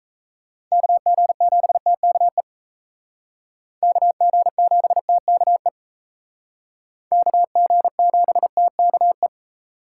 Morse code, heard as a keyed tone, about 34 wpm, 700 Hz. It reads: KG7TKE KG7TKE KG7TKE
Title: Speed Racing - USA 2x3 Call Signs 28wpm